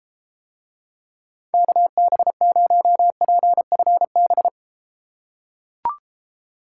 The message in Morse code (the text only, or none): KB0PFB